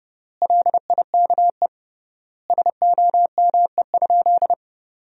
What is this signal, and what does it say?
Morse code, heard as a keyed tone, about 30 wpm, 700 Hz. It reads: LIKE SOME?
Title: 3x Word-Spacing - Sentences from Top 100 Words - 30wpm